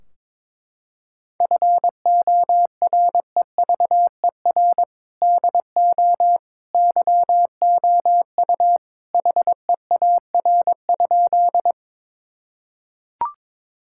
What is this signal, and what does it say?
Morse code, heard as a keyed tone, about 22 wpm, 700 Hz. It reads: FOREVER DO YOU HEAR?
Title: Sentences from Top 1600 Words 22wpm